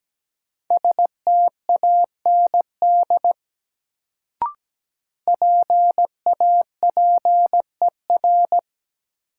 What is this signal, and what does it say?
Morse code, heard as a keyed tone, about 17 wpm, 700 Hz. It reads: STAND PAPER
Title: Sentences from Top 300 Words - New Word Drill - 17wpm